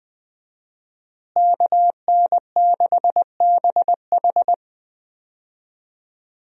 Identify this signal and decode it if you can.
Morse code, heard as a keyed tone, about 20 wpm, 700 Hz. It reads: KN6BH
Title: USA Call Signs - 5 Characters - Repeated 2x - 20wpm